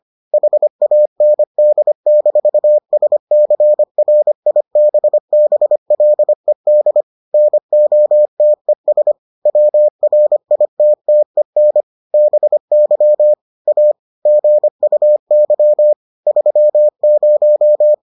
Morse code, heard as a keyed tone, about 25 wpm, 600 Hz. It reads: HAND-SCRIBBLED NOTES WRITTEN BY A GUY 30